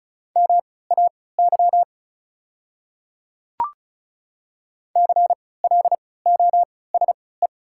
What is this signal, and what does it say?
Morse code, heard as a keyed tone, about 36 wpm, spaced as for 20 wpm, 690 Hz. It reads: MAY CLOSE
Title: 3x Character Spacing - Top 300 Words - 35wpm